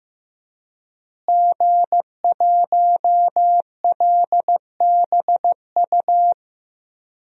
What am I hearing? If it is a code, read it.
Morse code, heard as a keyed tone, about 15 wpm, 700 Hz. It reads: G1LBU
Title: DXCC-100 Call Signs - 5 Characters - Repeated 2x - 15wpm